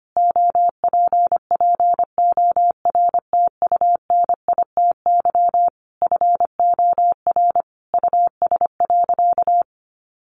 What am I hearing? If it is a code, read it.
Morse code, heard as a keyed tone, about 25 wpm, 700 Hz. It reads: OPPORTUNITY FOR US.
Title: Sentences from Top 1000 Words 25wpm